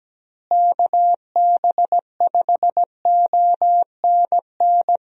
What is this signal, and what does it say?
Morse code, heard as a keyed tone, about 17 wpm, 700 Hz. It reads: KB5ONN
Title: USA Call Signs - 6 Characters - Repeated 3x - 17wpm